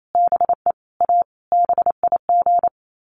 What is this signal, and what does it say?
Morse code, heard as a keyed tone, about 28 wpm, 700 Hz. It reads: BE A BIG
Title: Sets of 5 Words Encore2 28wpm